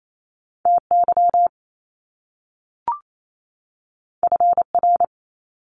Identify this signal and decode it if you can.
Morse code, heard as a keyed tone, about 28 wpm, 700 Hz. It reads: TY FR
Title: Top Two Letter Combinations in Words 28wpm